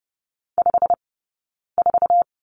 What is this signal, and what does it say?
Morse code, heard as a keyed tone, about 30 wpm, 700 Hz. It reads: 5 4
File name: 092 Lesson 16d - Letter c - Two Characters 30wpm